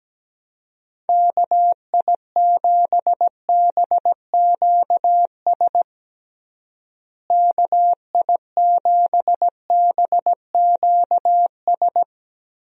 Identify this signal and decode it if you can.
Morse code, heard as a keyed tone, about 17 wpm, 700 Hz. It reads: KI7BQS KI7BQS
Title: USA Call Signs - 6 Characters - Repeated 3x - 17wpm